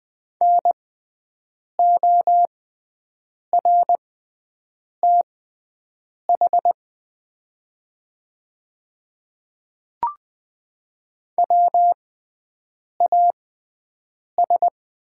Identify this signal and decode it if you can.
Morse code, heard as a keyed tone, about 20 wpm, spaced as for 7 wpm, 700 Hz. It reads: NORTH WAS
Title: 6x Character Spacing - Top 300 Words - 20wpm